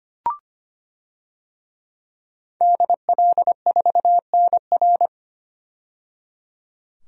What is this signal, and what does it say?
Morse code, heard as a keyed tone, about 25 wpm, 690 Hz. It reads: DL4NR
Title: DXCC-100 Call Signs - 5 Characters - 25wpm